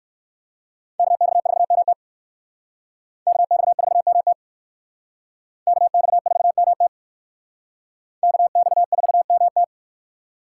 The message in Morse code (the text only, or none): KX4MT KX4MT KX4MT KX4MT